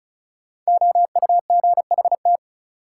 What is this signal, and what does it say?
Morse code, heard as a keyed tone, about 36 wpm, 700 Hz. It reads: OUGHT
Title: Sets of 5 Words Encore - New Word Drill - 35wpm